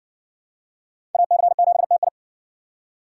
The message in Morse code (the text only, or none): AC7TI